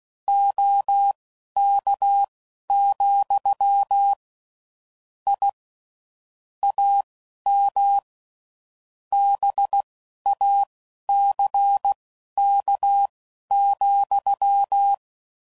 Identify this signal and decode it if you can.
Morse code, heard as a keyed tone, about 16 wpm, spaced as for 11 wpm, 800 Hz. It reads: OK, I AM BACK,
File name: CW Code Groups @10WPM / QSOs / QSO kc7haa